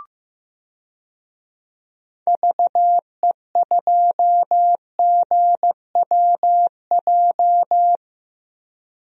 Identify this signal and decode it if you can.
Morse code, heard as a keyed tone, about 15 wpm, 700 Hz. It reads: VE2GWJ
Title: DXCC-100 Call Signs - 6 Characters - 15wpm